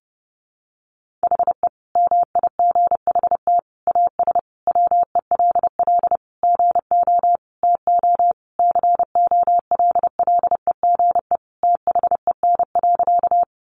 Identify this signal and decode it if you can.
Morse code, heard as a keyed tone, about 30 wpm, 700 Hz. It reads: HE MIGHT AS WELL GO TO COLLEGE THEN.
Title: Sentences from Top 1300 Words 30wpm